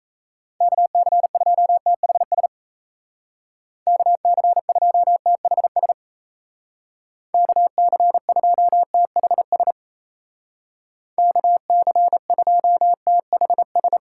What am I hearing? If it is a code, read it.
Morse code, heard as a keyed tone, about 36 wpm, 700 Hz. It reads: KC2THS KC2THS KC2THS KC2THS